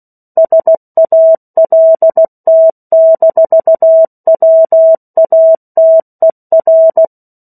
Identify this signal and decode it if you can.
Morse code, heard as a keyed tone, about 16 wpm, 650 Hz. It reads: SALT-WATER